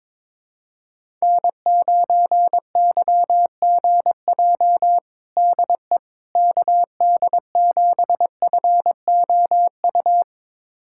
Morse code, heard as a keyed tone, about 22 wpm, 700 Hz. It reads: N9YGJ DE KD7FOU
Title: USA Callsign DE Callsign 22wpm